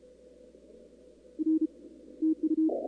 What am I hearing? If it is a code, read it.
Morse code, heard as a keyed tone, about 34 wpm, 350 Hz. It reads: R TU
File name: contests / 2019-cw-wpx-kt5j-k5pi- -2330 0-21